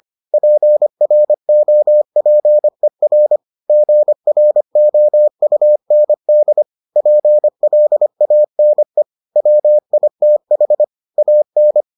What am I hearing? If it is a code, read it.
Morse code, heard as a keyed tone, about 25 wpm, 600 Hz. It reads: PROPER GROUND PLANE WITH AN